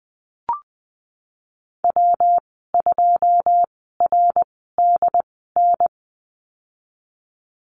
Morse code, heard as a keyed tone, about 20 wpm, spaced as for 15 wpm, 700 Hz. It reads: W2RDN
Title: 167 Lesson 27g - Number 7 - Call Signs 20wpm